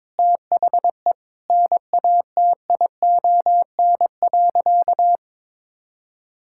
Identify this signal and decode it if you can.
Morse code, heard as a keyed tone, about 22 wpm, 700 Hz. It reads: THE NATION.
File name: practice / 22wpm / Sentences with 4-Word Phrases - 22wpm